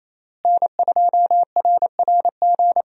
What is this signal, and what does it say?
Morse code, heard as a keyed tone, about 28 wpm, 700 Hz. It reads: N2RRG
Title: USA Call Signs - 5 Characters - Repeated 3x - 28wpm